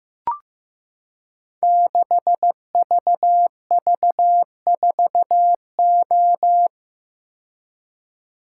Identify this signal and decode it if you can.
Morse code, heard as a keyed tone, about 15 wpm, 700 Hz. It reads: 6VV4O
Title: Code Groups - 5 Letter:Numbers - 15wpm